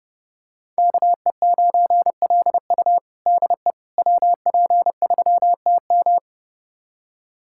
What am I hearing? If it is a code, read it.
Morse code, heard as a keyed tone, about 30 wpm, 700 Hz. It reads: KE9LU DE WP3TM
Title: USA Callsign DE Callsign - 5 Characters - 30wpm